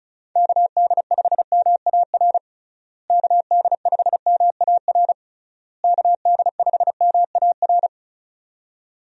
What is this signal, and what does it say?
Morse code, heard as a keyed tone, about 34 wpm, 700 Hz. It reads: KD5MAR KD5MAR KD5MAR
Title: USA Call Signs - 6 Characters - Repeated 3x - 35wpm